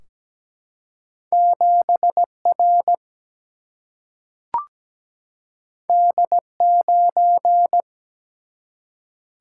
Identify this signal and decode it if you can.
Morse code, heard as a keyed tone, about 17 wpm, 700 Hz. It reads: 7R D9
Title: Single-Letter Single-Number 17wpm